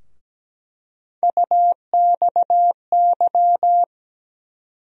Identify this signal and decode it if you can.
Morse code, heard as a keyed tone, about 17 wpm, 700 Hz. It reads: UXY
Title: USA Call Sign Postfixes - 3 Letters 17wpm